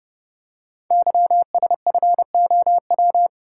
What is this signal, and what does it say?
Morse code, heard as a keyed tone, about 30 wpm, 700 Hz. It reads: YSFOW
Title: Code Groups - 5 Letter:Numbers - 30wpm